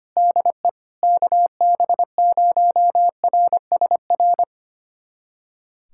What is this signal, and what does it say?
Morse code, heard as a keyed tone, about 25 wpm, 700 Hz. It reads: DE KB0RSR
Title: USA Callsign DE Callsign - 6 Characters - 25wpm